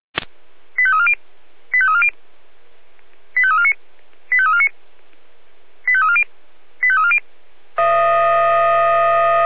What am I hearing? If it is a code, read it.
Five-tone selective call (ZVEI), heard as quick runs of five tones, tones of about 70 ms. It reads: 85340 85340 85349 85349 85340 85340